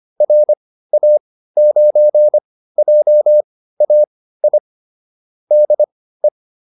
Morse code, heard as a keyed tone, about 25 wpm, spaced as for 15 wpm, 600 Hz. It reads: RA9JAI DE